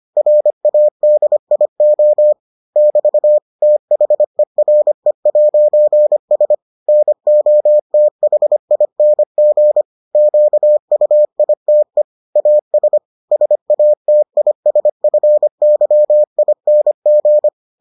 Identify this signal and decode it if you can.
Morse code, heard as a keyed tone, about 25 wpm, 600 Hz. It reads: RADIO <BT> THERE'S NOTHING QUITE AS SATISFYING